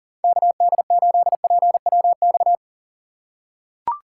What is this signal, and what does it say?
Morse code, heard as a keyed tone, about 40 wpm, 700 Hz. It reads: KD8PWX
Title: USA Call Signs - 6 Characters - 4x Character Spacing - 40wpm